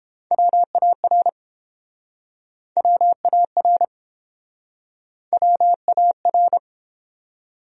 Morse code, heard as a keyed tone, about 31 wpm, 700 Hz. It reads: WAR WAR WAR